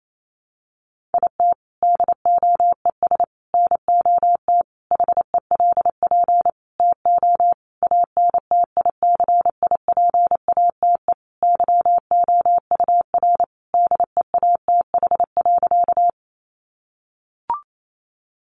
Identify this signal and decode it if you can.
Morse code, heard as a keyed tone, about 28 wpm, 700 Hz. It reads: IT DOES NOT HELP TO ANTICIPATE YOUR DEATH.
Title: Sentences from Top 1500 Words 28wpm